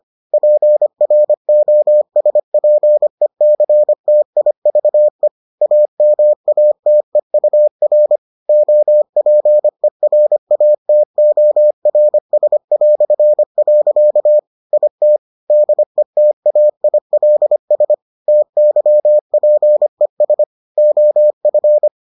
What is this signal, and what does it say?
Morse code, heard as a keyed tone, about 25 wpm, 600 Hz. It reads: PROSPECTIVE AMATEUR OPERATORS". IT DETAILS TYPES OF